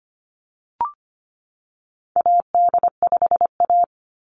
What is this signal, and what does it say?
Morse code, heard as a keyed tone, about 25 wpm, 690 Hz. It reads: AD5A
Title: NA SOTA Activator Callsigns 25wpm